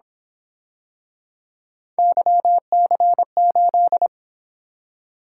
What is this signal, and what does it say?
Morse code, heard as a keyed tone, about 27 wpm, 700 Hz. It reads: YC8
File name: Speed-Racing - Three Letter-Number Combinations - 22wpm